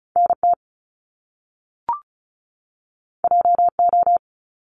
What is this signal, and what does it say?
Morse code, heard as a keyed tone, about 35 wpm, 700 Hz. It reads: NT JO
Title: Top Two Letter Combinations in Words 35wpm